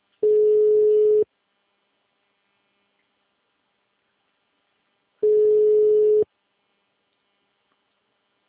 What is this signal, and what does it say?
Ringback tone (Europe, 425 Hz).
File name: out- -2002- - - .44